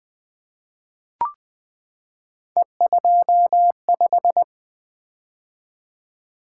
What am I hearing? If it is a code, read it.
Morse code, heard as a keyed tone, about 20 wpm, 700 Hz. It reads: E25
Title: Three Letter-Number Combinations - 20wpm